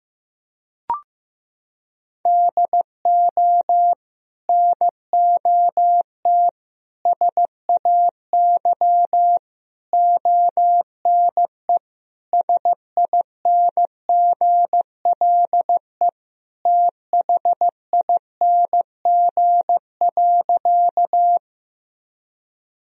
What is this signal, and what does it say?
Morse code, heard as a keyed tone, about 15 wpm, 700 Hz. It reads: DO NOT SAY ONE SINGLE THING.
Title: Sentences from Top 1000 Words - Review Part 4 - 15wpm